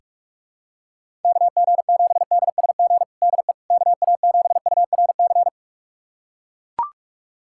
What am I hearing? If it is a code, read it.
Morse code, heard as a keyed tone, about 41 wpm, 700 Hz. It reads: KG7DSG DE KA7URC
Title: USA Callsign DE Callsign - 6 Characters - 45wpm